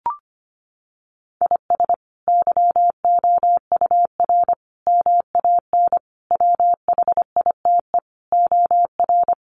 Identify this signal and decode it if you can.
Morse code, heard as a keyed tone, about 25 wpm, 700 Hz. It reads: IS YOUR MAN WHITE OR
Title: Sentences from Top 400 Words 25wpm